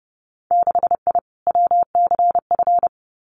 Morse code, heard as a keyed tone, about 30 wpm, 700 Hz. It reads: 6I WCF
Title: Winter Field Day Exchanges Encore 30wpm